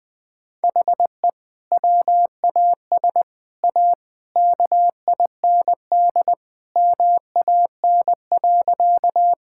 Morse code, heard as a keyed tone, about 20 wpm, 700 Hz. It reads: HE WAS A KIND MAN.
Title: Sentences with 3-Word Phrases - 20wpm